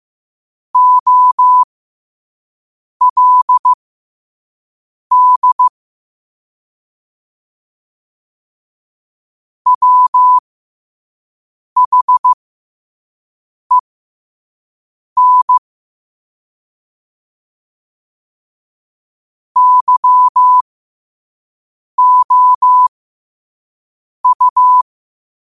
Morse code, heard as a keyed tone, about 15 wpm, spaced as for 5 wpm, 1000 Hz. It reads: OLD WHEN YOU
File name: qotdfiles / 5wpm